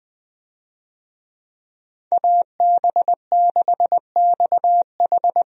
Morse code, heard as a keyed tone, about 20 wpm, 700 Hz. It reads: AB6XH
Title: USA Call Signs - 5 Characters - 20wpm